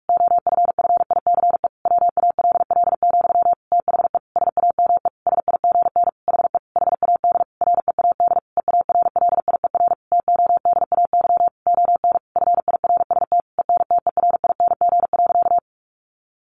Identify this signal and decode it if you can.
Morse code, heard as a keyed tone, about 44 wpm, 700 Hz. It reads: OFFICE WALL, THE SAME SIGN HE HAD READ EARLIER TODAY ON FIRST ENTERING.
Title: A Sound Of Thunder 45wpm